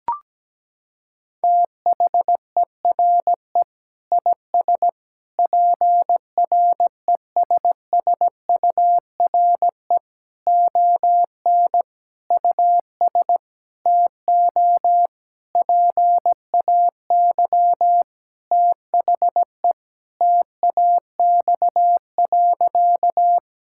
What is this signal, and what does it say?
Morse code, heard as a keyed tone, about 17 wpm, 700 Hz. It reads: THERE IS PRESSURE ON US TO PAY THE TAX.
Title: Sentences from Top 1000 Words - Review Part 6 - 17wpm